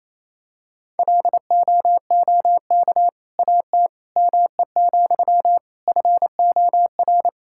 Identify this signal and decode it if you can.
Morse code, heard as a keyed tone, about 28 wpm, 700 Hz. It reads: LOOK AT ME, FOR
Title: Sentences from Top 1000 Words - Review Part 5 - 28wpm